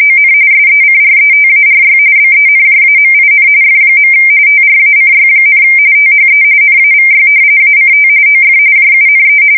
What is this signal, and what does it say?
RTTY, about 45.45 baud. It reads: ON SECRETARY BENJAMIN F. TRACY, THE NAVY ACT OF JUNE
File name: fsk170 / battleships